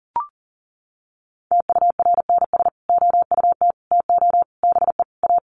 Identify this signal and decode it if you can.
Morse code, heard as a keyed tone, about 40 wpm, 700 Hz. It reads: TURNS OUT TO BE A
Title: Sets of 5 Words Encore 40wpm